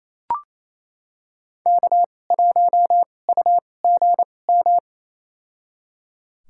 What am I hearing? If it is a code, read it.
Morse code, heard as a keyed tone, about 28 wpm, spaced as for 20 wpm, 700 Hz. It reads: K1UGM
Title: USA Call Signs - 5 Characters - 2x Character Spacing - 28wpm